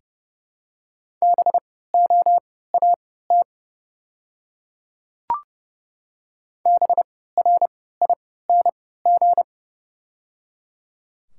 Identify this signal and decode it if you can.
Morse code, heard as a keyed tone, about 31 wpm, spaced as for 17 wpm, 700 Hz. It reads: BOAT BRING
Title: 3x Character Spacing - Top 500 Words - 30wpm